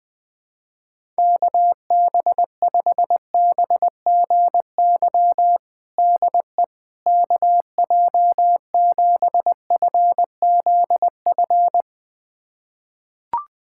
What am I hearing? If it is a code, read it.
Morse code, heard as a keyed tone, about 20 wpm, 700 Hz. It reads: KB5BGY DE KJ7FZF
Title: USA Callsign DE Callsign 20wpm